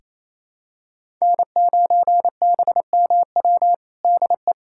Morse code, heard as a keyed tone, about 28 wpm, 700 Hz. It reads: N9BMW DE
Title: USA Callsign DE Callsign 28wpm